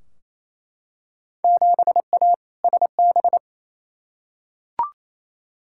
Morse code, heard as a keyed tone, about 28 wpm, 700 Hz. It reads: 7A SB
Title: Field Day Exchanges Encore - 28wpm